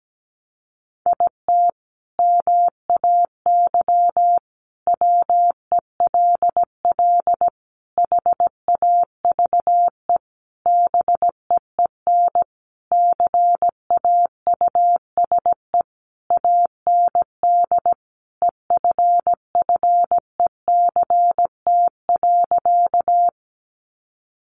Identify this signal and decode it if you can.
Morse code, heard as a keyed tone, about 17 wpm, 700 Hz. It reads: IT MAY WELL HAVE BEEN CAUSE AND EFFECT.
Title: Sentences from Top 1000 Words 17wpm